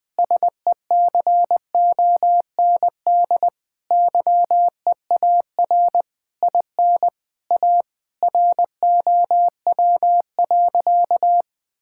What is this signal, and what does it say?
Morse code, heard as a keyed tone, about 20 wpm, 700 Hz. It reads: SECOND YEAR IN A ROW.
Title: Sentences with 7-Word Phrases - Spoken Beginning - 20wpm